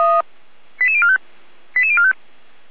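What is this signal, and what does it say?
Five-tone selective call (ZVEI), heard as quick runs of five tones, tones of about 70 ms. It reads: 88045 88045
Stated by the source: Received on an SDR by feuerwehr-orlamuende (fire brigade Orlamünde).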